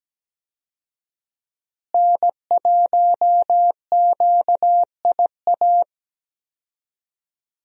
Morse code, heard as a keyed tone, about 17 wpm, 700 Hz. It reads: N1QIA
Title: USA Call Signs - 5 Characters - Repeated 3x - 17wpm